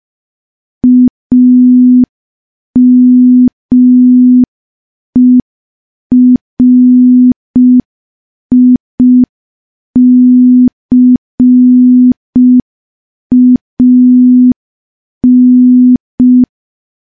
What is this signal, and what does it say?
Morse code, heard as a keyed tone, about 5 wpm, 250 Hz. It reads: AMERICAN
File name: morse / morse-training-05WPM